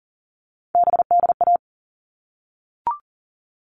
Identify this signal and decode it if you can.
Morse code, heard as a keyed tone, about 40 wpm, 700 Hz. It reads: BDA
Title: All Cut Numbers - 3-Digit 40wpm